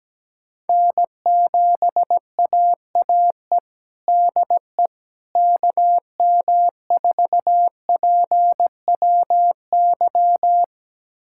Morse code, heard as a keyed tone, about 17 wpm, 700 Hz. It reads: N7AAE DE KM4PWY
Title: USA Callsign DE Callsign 17wpm